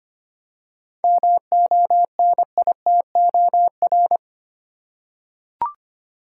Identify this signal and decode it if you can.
Morse code, heard as a keyed tone, about 25 wpm, 700 Hz. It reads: MONITOR
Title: Seven Letter Words - 25wpm